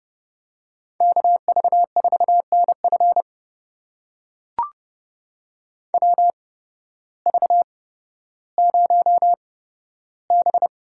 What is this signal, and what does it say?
Morse code, heard as a keyed tone, about 32 wpm, spaced as for 14 wpm, 700 Hz. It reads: KV4NF WV0B
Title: USA Call Signs - 8x Character Spacing - 30wpm